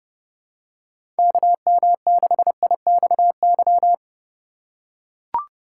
Morse code, heard as a keyed tone, about 30 wpm, 700 Hz. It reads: KM6IXY
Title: USA Call Signs - 6 Characters - 4x Character Spacing - 30wpm